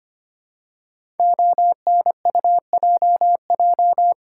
Morse code, heard as a keyed tone, about 25 wpm, 700 Hz. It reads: ONUJJ
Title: Code Groups - 5 Letters - 25wpm